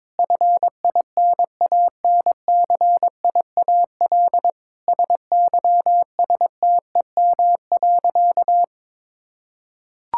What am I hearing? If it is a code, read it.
Morse code, heard as a keyed tone, about 22 wpm, 700 Hz. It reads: FINANCIAL SYSTEM.
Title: Sentences from Top 1000 Words - Review Part 5 - 22wpm